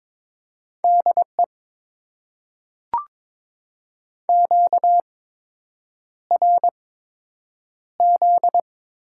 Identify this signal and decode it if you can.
Morse code, heard as a keyed tone, about 22 wpm, spaced as for 8 wpm, 700 Hz. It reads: DE QRZ
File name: Top 90 QSO Elements - 8x Character Spacing - 22wpm